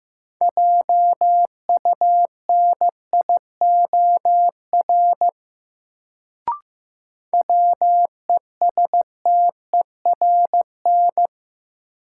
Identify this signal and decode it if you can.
Morse code, heard as a keyed tone, about 15 wpm, 700 Hz. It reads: JUNIOR WESTERN
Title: Sentences from Top 1000 Words - Review Part 7 - New Word Drill - 15wpm